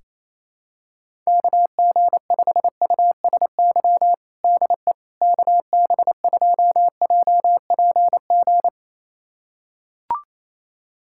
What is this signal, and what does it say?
Morse code, heard as a keyed tone, about 28 wpm, 700 Hz. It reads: KG5USY DE KB2JPG
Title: USA Callsign DE Callsign - 6 Characters - 28wpm